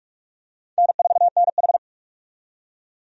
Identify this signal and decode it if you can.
Morse code, heard as a keyed tone, about 46 wpm, 700 Hz. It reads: N4NH
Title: USA Call Signs - 4 Characters - 45wpm